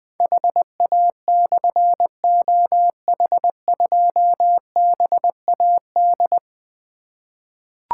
Morse code, heard as a keyed tone, about 20 wpm, 700 Hz. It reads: HA/OH2BAD
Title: Difficult Call Signs 20wpm